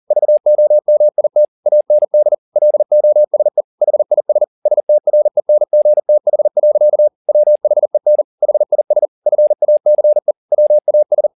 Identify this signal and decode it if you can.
Morse code, heard as a keyed tone, about 40 wpm, 600 Hz. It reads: VOMIT AND LOSE HIS STRENGTH. WHEN HIS FACE WAS
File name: the old man and the sea 2 40wpm